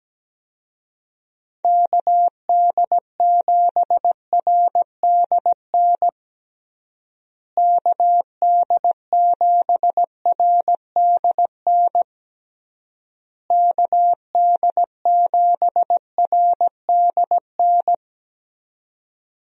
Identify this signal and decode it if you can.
Morse code, heard as a keyed tone, about 17 wpm, 700 Hz. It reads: KD7RDN KD7RDN KD7RDN
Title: USA Call Signs - 6 Characters - Repeated 3x - 17wpm